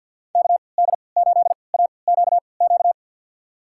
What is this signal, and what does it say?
Morse code, heard as a keyed tone, about 47 wpm, spaced as for 28 wpm, 700 Hz. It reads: KD8AQQ
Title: USA Call Signs - 6 Characters - 3x Character Spacing - 50wpm